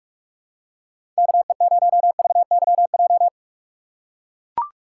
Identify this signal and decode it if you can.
Morse code, heard as a keyed tone, about 44 wpm, 700 Hz. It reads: KE0VYJ
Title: USA Call Signs - 3x Character Spacing - 45wpm